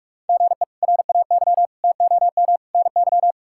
Morse code, heard as a keyed tone, about 41 wpm, 700 Hz. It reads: GE RAY TOM NY
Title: K1USN SST QSO Practice - 45wpm